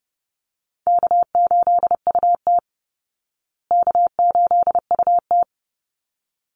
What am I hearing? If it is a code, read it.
Morse code, heard as a keyed tone, about 30 wpm, 700 Hz. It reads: K8UT K8UT
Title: Easy Call Signs Repeated 2x - 30wpm